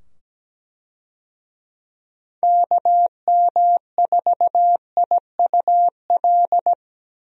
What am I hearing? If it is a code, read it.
Morse code, heard as a keyed tone, about 17 wpm, 700 Hz. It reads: KM4IUL
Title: USA Call Signs - 6 Characters - 17wpm